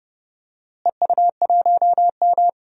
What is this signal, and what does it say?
Morse code, heard as a keyed tone, about 29 wpm, 700 Hz. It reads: EU1M
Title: DXCC-100 Call Signs - 4 Characters - Repeated 2x - 30wpm